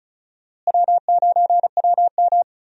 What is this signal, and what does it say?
Morse code, heard as a keyed tone, about 34 wpm, 700 Hz. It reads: W9WM
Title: USA Call Signs - 4 Characters - Repeated 3x - 35wpm